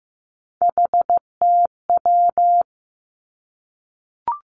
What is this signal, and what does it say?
Morse code, heard as a keyed tone, about 15 wpm, 700 Hz. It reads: HTW
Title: Three Challenging Letter-Number Combinations 15wpm